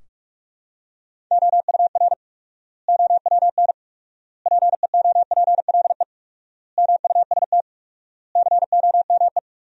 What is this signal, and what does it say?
Morse code, heard as a keyed tone, about 46 wpm, 700 Hz. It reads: OUR OWN PEOPLE MUST COME
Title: 4x Word-Spacing - Sentences from Top 300 Words - 45wpm